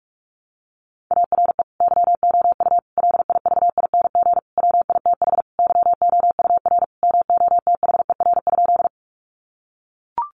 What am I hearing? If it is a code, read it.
Morse code, heard as a keyed tone, about 44 wpm, 700 Hz. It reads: ARE YOU LIVING WITH YOUR MOTHER?
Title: Sentences from Top 1100 Words 45wpm